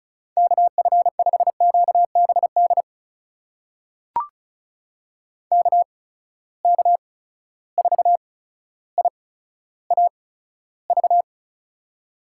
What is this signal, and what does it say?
Morse code, heard as a keyed tone, about 36 wpm, spaced as for 15 wpm, 700 Hz. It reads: KF5QBD KK4IAV
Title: USA Call Signs - 6 Characters - 8x Character Spacing - 35wpm